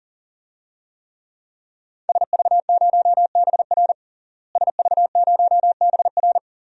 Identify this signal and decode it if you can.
Morse code, heard as a keyed tone, about 40 wpm, 700 Hz. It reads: SV0BR SV0BR
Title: DXCC-100 Call Signs - 5 Characters - Repeated 3x - 40wpm